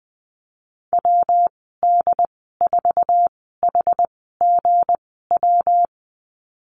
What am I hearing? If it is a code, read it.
Morse code, heard as a keyed tone, about 20 wpm, spaced as for 15 wpm, 700 Hz. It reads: WD4HGW
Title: 219 Lesson 35g - Number 0 - Call Signs 20wpm